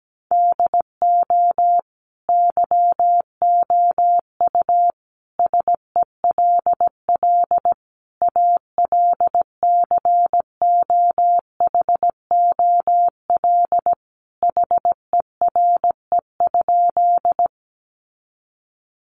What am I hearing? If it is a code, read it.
Morse code, heard as a keyed tone, about 17 wpm, 700 Hz. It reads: DO YOU SELL ALCOHOL HERE?
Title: Sentences from Top 1400 Words 17wpm